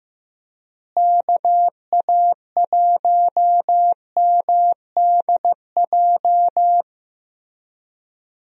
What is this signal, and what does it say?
Morse code, heard as a keyed tone, about 15 wpm, 700 Hz. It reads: KA1MDJ
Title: USA Call Signs - 6 Characters - Repeated 3x - 15wpm